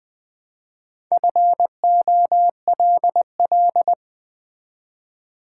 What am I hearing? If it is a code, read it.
Morse code, heard as a keyed tone, about 20 wpm, 690 Hz. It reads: FOLL
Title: Build A Word - 2 Letters At A Time - 6-Letter Words - 20wpm